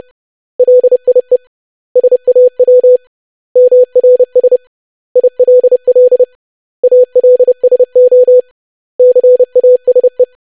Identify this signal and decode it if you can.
Morse code, heard as a keyed tone, about 30 wpm, 500 Hz. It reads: LIE SAW MRS ILL ALSO CASE
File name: lesson / web / basic500-25wpm